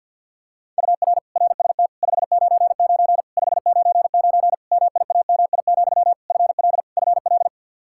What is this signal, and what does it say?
Morse code, heard as a keyed tone, about 53 wpm, 700 Hz. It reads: UR RST 599 599 MIAMI, FL FL
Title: Practice QSO - Exchange 1 - Beginning - 50wpm